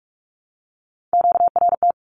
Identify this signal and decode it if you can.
Morse code, heard as a keyed tone, about 44 wpm, 700 Hz. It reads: QRT
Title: Q-Codes 45wpm